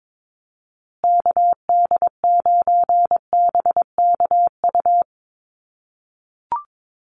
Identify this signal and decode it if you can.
Morse code, heard as a keyed tone, about 22 wpm, 700 Hz. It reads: KD9BKU
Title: USA Call Signs Repeated 2x 22wpm